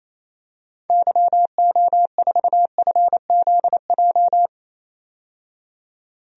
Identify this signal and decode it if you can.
Morse code, heard as a keyed tone, about 28 wpm, 700 Hz. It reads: YO4FZJ
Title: DXCC-100 Call Signs - 6 Characters - 28wpm